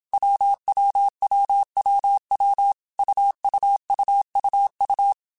Morse code, heard as a keyed tone, about 27 wpm, 800 Hz. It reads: WWWWW UUUUU
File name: Basic Sets / Basic Letter Groups Collective7 / AWUNDG@25